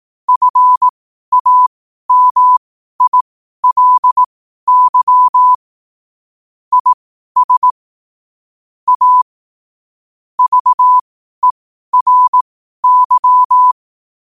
Morse code, heard as a keyed tone, about 18 wpm, spaced as for 12 wpm, 1000 Hz. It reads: FAMILY IS A VERY
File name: qotdfiles / 12wpm